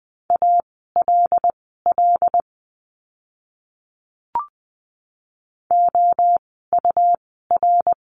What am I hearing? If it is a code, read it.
Morse code, heard as a keyed tone, about 20 wpm, spaced as for 15 wpm, 700 Hz. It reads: ALL OUR
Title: 139 Lesson 23f - Letter f - Words 20wpm